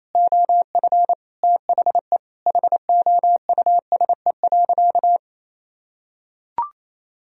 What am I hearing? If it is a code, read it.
Morse code, heard as a keyed tone, about 28 wpm, 700 Hz. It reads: OF THE HOUSE.
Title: Sentences from Top 400 Words and Abbreviations 28wpm